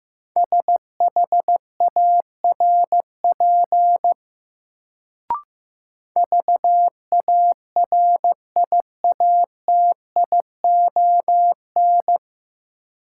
Morse code, heard as a keyed tone, about 15 wpm, 700 Hz. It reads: SHARP VARIATION
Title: Sentences from Top 1600 Words - New Word Drill Alt - 15wpm